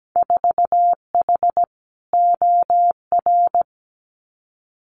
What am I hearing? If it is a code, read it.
Morse code, heard as a keyed tone, about 17 wpm, 700 Hz. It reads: 4H OR
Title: Winter Field Day Exchanges Encore 17wpm